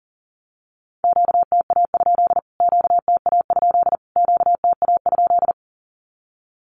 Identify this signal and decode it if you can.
Morse code, heard as a keyed tone, about 40 wpm, 700 Hz. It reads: QTA? QTA? QTA?
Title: Q-Codes Repeated 3x 40wpm